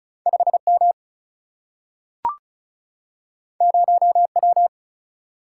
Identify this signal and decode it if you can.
Morse code, heard as a keyed tone, about 36 wpm, 700 Hz. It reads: 5 M 0 W
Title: Single-Letter Single-Number 35wpm